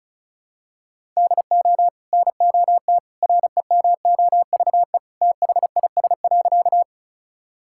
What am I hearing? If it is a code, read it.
Morse code, heard as a keyed tone, about 36 wpm, 700 Hz. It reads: DO NOT REMOVE THIS.
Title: Sentences from Top 1000 Words - Review Part 5 - 35wpm